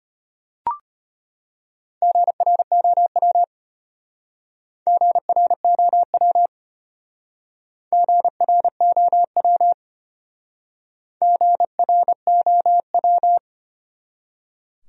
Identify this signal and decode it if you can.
Morse code, heard as a keyed tone, about 33 wpm, 700 Hz. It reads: GROW GROW GROW GROW